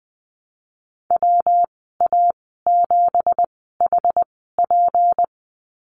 Morse code, heard as a keyed tone, about 20 wpm, spaced as for 15 wpm, 700 Hz. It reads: WA7HP
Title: 193 Lesson 31g - Letter v - Call Signs 20wpm